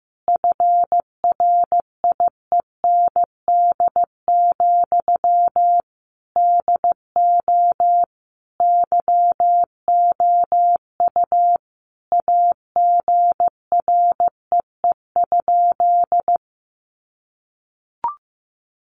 Morse code, heard as a keyed tone, about 15 wpm, 700 Hz. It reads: FRIEND, DO YOU AGREE?
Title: Sentences from Top 1100 Words 15wpm